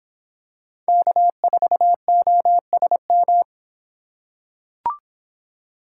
Morse code, heard as a keyed tone, about 24 wpm, 700 Hz. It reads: K4OSM